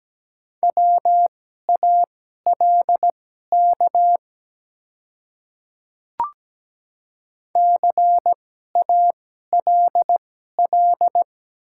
Morse code, heard as a keyed tone, about 17 wpm, spaced as for 12 wpm, 700 Hz. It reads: WALK CALL
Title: 2x Character Spacing - Top 300 Words - 17wpm